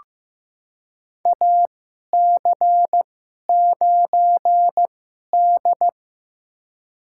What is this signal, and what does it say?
Morse code, heard as a keyed tone, about 15 wpm, spaced as for 11 wpm, 700 Hz. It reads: AC9D
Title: USA Call Signs - 2x Character Spacing - 15wpm